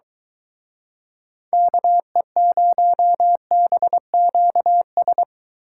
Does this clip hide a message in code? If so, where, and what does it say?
Yes: Morse code from 1.5 to 5.2 s, heard as a keyed tone, about 23 wpm, 700 Hz. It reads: KE0BQS